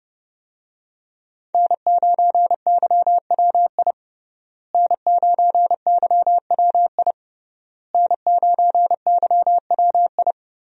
Morse code, heard as a keyed tone, about 30 wpm, 700 Hz. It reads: N9YWI N9YWI N9YWI
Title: USA Call Signs - 5 Characters - Repeated 3x - 30wpm